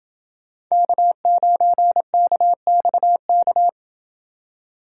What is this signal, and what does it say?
Morse code, heard as a keyed tone, about 28 wpm, 700 Hz. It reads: K9KXK